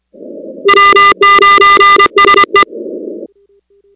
Morse code, heard as a keyed tone, about 24 wpm, 1200 Hz, with noise. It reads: W9SE
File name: contests / 2019-cw-wpx-kt5j-k5pi- -2330 0-19